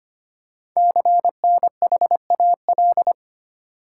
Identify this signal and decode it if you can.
Morse code, heard as a keyed tone, about 25 wpm, 700 Hz. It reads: CNHAL
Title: Code Groups - 5 Letters - 25wpm